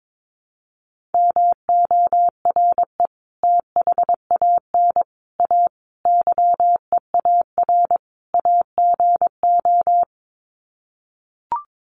Morse code, heard as a keyed tone, about 22 wpm, 700 Hz. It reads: MORE THAN A YEAR AGO
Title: Sets of 5 Words Encore2 22wpm